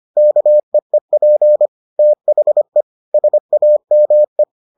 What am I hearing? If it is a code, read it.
Morse code, heard as a keyed tone, about 25 wpm, 600 Hz. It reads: KEEP THE SAME